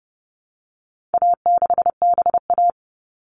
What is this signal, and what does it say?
Morse code, heard as a keyed tone, about 29 wpm, 700 Hz. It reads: A6BA
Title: All Cut Numbers - 4-Digit 30wpm